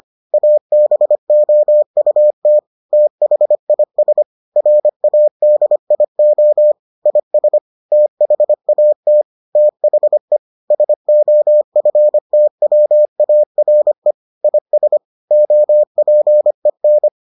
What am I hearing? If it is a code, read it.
Morse code, heard as a keyed tone, about 25 wpm, 600 Hz. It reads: ABOUT THIS RADIO IS THAT THE SOFTWARE IS OPEN